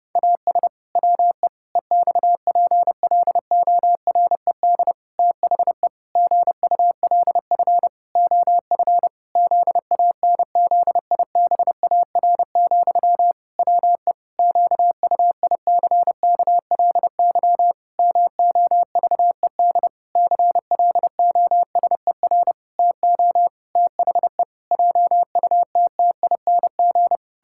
Morse code, heard as a keyed tone, about 30 wpm, 700 Hz. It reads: AS WE EXPLORED THE GULF OF ZANZIBAR, WE QUICKLY MOVED CLOSER TO THE JUTTING
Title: Pangram Challenge v2 - 30wpm